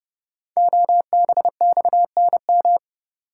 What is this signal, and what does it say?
Morse code, heard as a keyed tone, about 29 wpm, 700 Hz. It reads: OBXNM
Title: Code Groups - 5 Letters - 30wpm